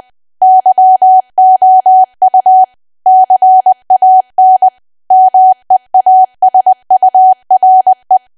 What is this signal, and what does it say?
Morse code, heard as a keyed tone, about 20 wpm, 750 Hz. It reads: YOU CAN MEASURE
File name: cwa / W1AW-Level3 / 20WPM